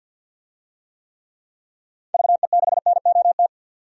Morse code, heard as a keyed tone, about 52 wpm, 700 Hz. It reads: VE6NOT
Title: DXCC-100 Call Signs - 6 Characters - 50wpm